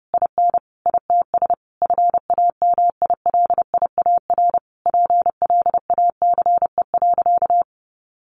Morse code, heard as a keyed tone, about 30 wpm, 700 Hz. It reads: IN ITS FAMILIAR PLACE.
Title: Sentences from Top 1500 Words 30wpm